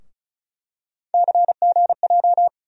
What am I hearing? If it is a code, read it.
Morse code, heard as a keyed tone, about 36 wpm, 700 Hz. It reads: CGJ
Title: USA Call Sign Postfixes - 3 Letters 35wpm